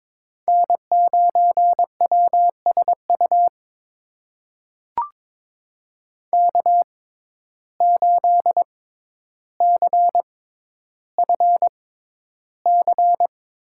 Morse code, heard as a keyed tone, about 22 wpm, spaced as for 11 wpm, 700 Hz. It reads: N9WSU K8CFC
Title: USA Call Signs - 5 Characters - 6x Character Spacing - 22wpm